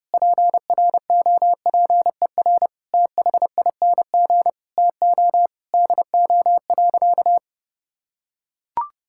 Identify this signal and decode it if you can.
Morse code, heard as a keyed tone, about 30 wpm, 700 Hz. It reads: PROPER THING TO DO.
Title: Sentences from Top 1000 Words - Review Part 5 - 30wpm